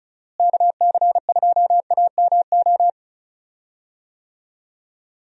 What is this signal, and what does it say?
Morse code, heard as a keyed tone, about 35 wpm, 700 Hz. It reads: KC2AMO
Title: USA Call Signs - 6 Characters - Repeated 2x - 35wpm